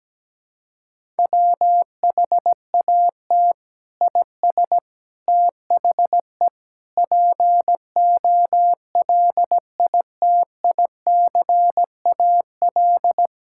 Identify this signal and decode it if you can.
Morse code, heard as a keyed tone, about 17 wpm, 700 Hz. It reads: WHAT IS THE POLITICAL
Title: Sentences from Top 1000 Words - Review Part 5 - 17wpm